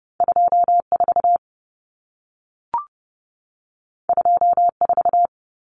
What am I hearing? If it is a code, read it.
Morse code, heard as a keyed tone, about 30 wpm, 700 Hz. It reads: 24 24
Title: Numbers - 2-Digit 30wpm